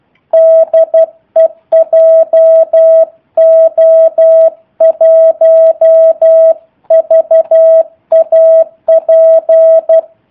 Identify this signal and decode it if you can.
Morse code, heard as a keyed tone, about 12 wpm, 650 Hz. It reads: DE JO1VAP
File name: jh8cbh / 2019cw / sp03- w